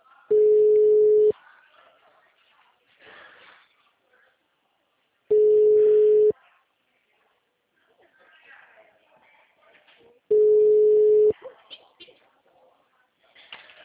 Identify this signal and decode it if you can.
Ringback tone (Europe, 425 Hz), with speech.
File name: out- -2002- - - .83873